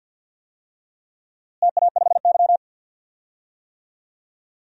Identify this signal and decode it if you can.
Morse code, heard as a keyed tone, about 51 wpm, 700 Hz. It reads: TA5Y
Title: DXCC-100 Call Signs - 4 Characters - Repeated 2x - 50wpm